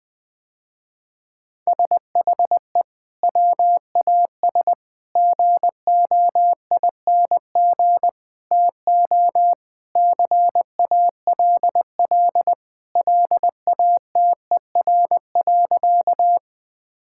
Morse code, heard as a keyed tone, about 20 wpm, 700 Hz. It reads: SHE WAS GOING TO CALL LATER.
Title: Sentences with 4-Word Phrases - Spoken Beginning - 20wpm